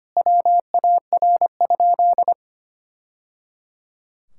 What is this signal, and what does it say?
Morse code, heard as a keyed tone, about 25 wpm, 700 Hz. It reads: WAR?
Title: Sentences from Top 1000 Words - Review Part 5 - 25wpm